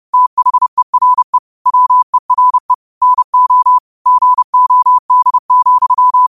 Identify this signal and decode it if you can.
Morse code, heard as a keyed tone, about 30 wpm, 1000 Hz. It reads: THERE WERE NO GOD,
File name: qotdfiles / 30wpm